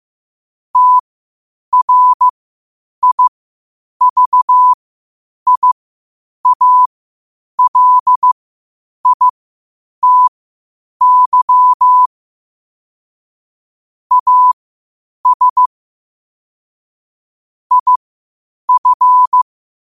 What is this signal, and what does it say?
Morse code, heard as a keyed tone, about 15 wpm, spaced as for 8 wpm, 1000 Hz. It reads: TRIVIALITY AS IF
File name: qotdfiles / 8wpm